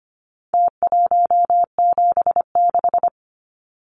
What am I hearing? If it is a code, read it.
Morse code, heard as a keyed tone, about 25 wpm, 700 Hz. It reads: T176
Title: Numbers - Contest Style - 4-Digit 25wpm